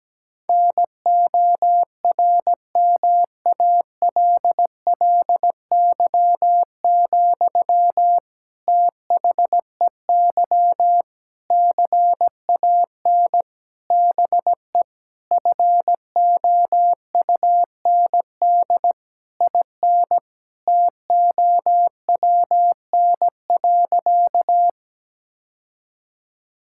Morse code, heard as a keyed tone, about 17 wpm, 700 Hz. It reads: NORMALLY, THEY CAN BE FOUND IN TOWN.
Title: Sentences from Top 1000 Words - Review Part 6 - 17wpm